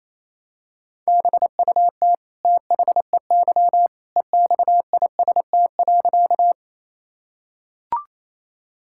Morse code, heard as a keyed tone, about 28 wpm, 700 Hz. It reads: BUT THEY EXIST.
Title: Sentences from Top 1000 Words - Review Part 5 - 28wpm